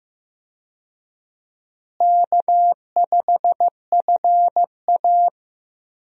Morse code, heard as a keyed tone, about 15 wpm, 700 Hz. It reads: K5FA
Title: USA Call Signs - 4 Characters - Repeated 2x - 15wpm